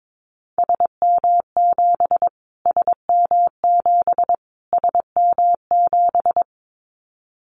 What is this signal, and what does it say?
Morse code, heard as a keyed tone, about 22 wpm, 700 Hz. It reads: SM7 SM7 SM7
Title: Callsign Prefixes for Top 25 DXCC - Repeated 3x - 22wpm